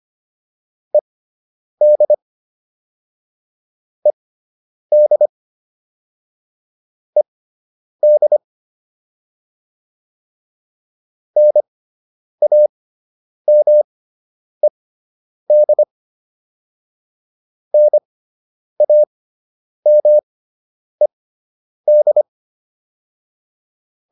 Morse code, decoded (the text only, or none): ED ED ED NAMED NAMED